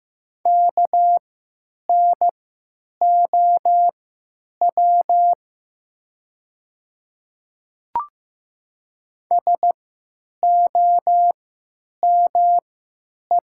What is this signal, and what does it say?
Morse code, heard as a keyed tone, about 15 wpm, spaced as for 9 wpm, 700 Hz. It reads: KNOW SOME
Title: 3x Character Spacing - Top 100 Words - 15wpm